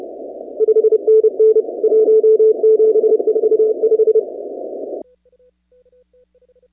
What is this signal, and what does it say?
Morse code, heard as a keyed tone, about 30 wpm, 440 Hz. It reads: 5NN 1745
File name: 2019-cw-wpx-kt5j-k5pi